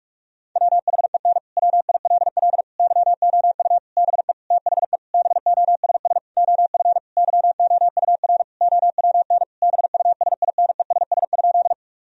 Morse code, heard as a keyed tone, about 41 wpm, 700 Hz. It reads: WHEN WILL YOU BE THE BOSS OF YOUR OWN BUSINESS?
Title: Sentences from Top 1000 Words - Review Part 4 - 45wpm